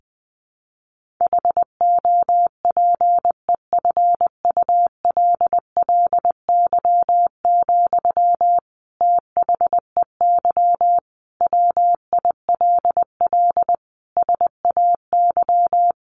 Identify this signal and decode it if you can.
Morse code, heard as a keyed tone, about 20 wpm, 700 Hz. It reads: HOPEFULLY, THEY WILL SAY
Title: Sentences from Top 1500 Words 20wpm